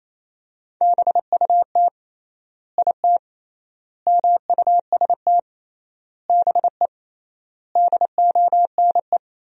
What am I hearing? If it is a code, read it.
Morse code, heard as a keyed tone, about 28 wpm, 700 Hz. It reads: BUT IT MUST BE DONE
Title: 3x Word-Spacing - Sentences from Top 500 Words - 28wpm